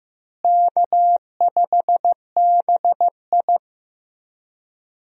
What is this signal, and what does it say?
Morse code, heard as a keyed tone, about 15 wpm, 700 Hz. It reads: K5BI
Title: USA Call Signs - 4 Characters - Repeated 2x - 15wpm